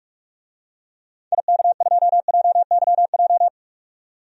Morse code, heard as a keyed tone, about 46 wpm, 700 Hz. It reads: IK2JYJ
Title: DXCC-100 Call Signs - 6 Characters - 45wpm